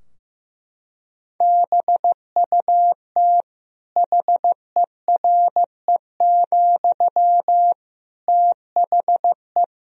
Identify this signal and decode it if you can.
Morse code, heard as a keyed tone, about 16 wpm, 700 Hz. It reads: BUT HERE, THE
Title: Sentences from Top 1000 Words - Review Part 3 - 15wpm